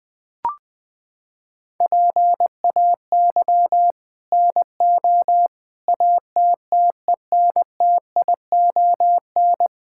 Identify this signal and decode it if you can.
Morse code, heard as a keyed tone, about 21 wpm, 700 Hz. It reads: PAY NO ATTENTION
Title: Sentences from Top 1000 Words - Review Part 6 - 20wpm